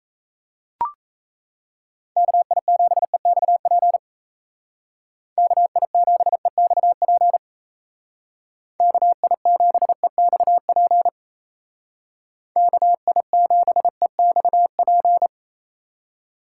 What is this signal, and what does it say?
Morse code, heard as a keyed tone, about 36 wpm, 700 Hz. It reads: KI7EXP KI7EXP KI7EXP KI7EXP